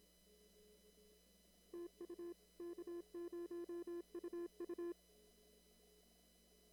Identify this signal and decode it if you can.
Morse code, heard as a keyed tone, about 26 wpm, 340 Hz. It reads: TU K0UU
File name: CW / kt5j wpx cw-op-k5pi